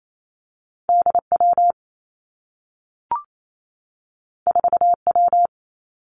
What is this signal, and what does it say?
Morse code, heard as a keyed tone, about 28 wpm, 700 Hz. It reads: D W 4 W
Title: Two Challenging Letter-Number Combinations 28wpm